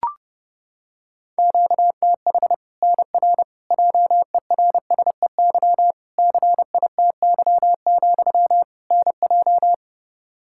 Morse code, heard as a keyed tone, about 30 wpm, 700 Hz. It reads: QTH NR JERSEY CITY, NJ
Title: K1USN SST QSO Practice 30wpm